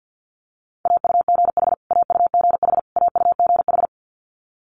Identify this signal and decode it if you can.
Morse code, heard as a keyed tone, about 50 wpm, 700 Hz. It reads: AUGH AUGH AUGH
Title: Multi-letter Phonemes Repeated 3x 50wpm